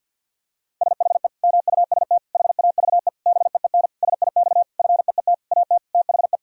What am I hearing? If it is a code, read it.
Morse code, heard as a keyed tone, about 52 wpm, 700 Hz. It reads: SHE MUST HAVE BEEN SIX FEET AT THE
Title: Sentences from Top 1000 Words - Review Part 2 - 50wpm